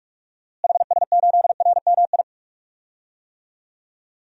Morse code, heard as a keyed tone, about 41 wpm, 700 Hz. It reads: HS8RMI
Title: DXCC-100 Call Signs - 6 Characters - Repeated 3x - 45wpm